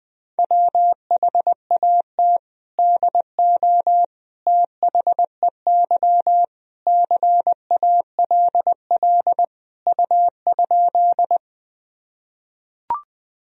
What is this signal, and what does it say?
Morse code, heard as a keyed tone, about 21 wpm, 700 Hz. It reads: WHAT DO THEY CALL U?
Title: Sentences from Top 100 Words and Abbreviations 20wpm